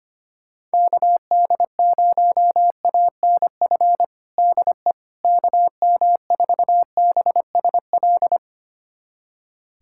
Morse code, heard as a keyed tone, about 25 wpm, 700 Hz. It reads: KD0ANF DE KM4BSL
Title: USA Callsign DE Callsign - 6 Characters - 25wpm